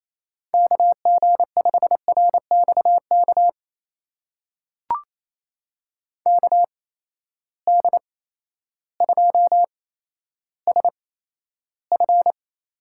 Morse code, heard as a keyed tone, about 28 wpm, spaced as for 13 wpm, 700 Hz. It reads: KG5RXK KD2SF
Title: USA Call Signs - 6 Characters - 8x Character Spacing - 28wpm